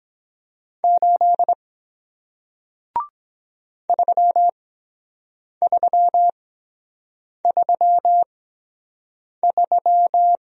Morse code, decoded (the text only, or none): 8 3333